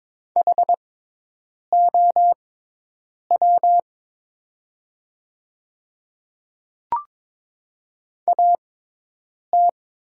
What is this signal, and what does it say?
Morse code, heard as a keyed tone, about 22 wpm, spaced as for 8 wpm, 700 Hz. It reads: HOW AT
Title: 6x Character Spacing - Top 100 Words - 22wpm